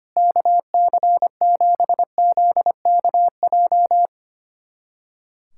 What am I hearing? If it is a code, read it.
Morse code, heard as a keyed tone, about 25 wpm, 700 Hz. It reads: KC7ZKJ
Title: USA Callsign DE Callsign - 6 Characters - 25wpm